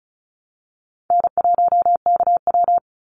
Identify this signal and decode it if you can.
Morse code, heard as a keyed tone, about 34 wpm, 700 Hz. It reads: N1KW
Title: Call Signs Repeated 2x - 35wpm